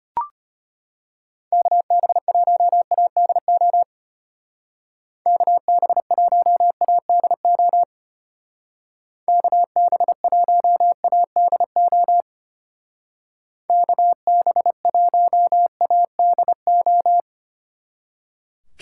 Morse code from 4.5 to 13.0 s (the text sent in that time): KB1ADO KB1ADO